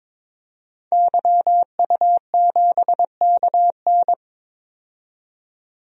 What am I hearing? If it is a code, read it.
Morse code, heard as a keyed tone, about 22 wpm, 700 Hz. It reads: YU7KN
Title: DXCC-100 Call Signs - 5 Characters - 22wpm